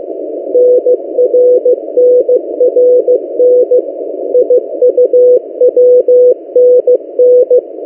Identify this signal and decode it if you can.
Morse code, heard as a keyed tone, about 15 wpm, 500 Hz. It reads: NRNRN IUWNN
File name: M08a- -1355utc-8096khz